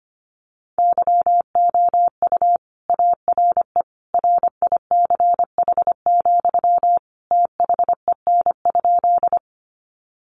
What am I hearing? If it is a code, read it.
Morse code, heard as a keyed tone, about 25 wpm, 700 Hz. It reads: YOU ARE RICH, THEN?
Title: Sentences from Top 1100 Words 25wpm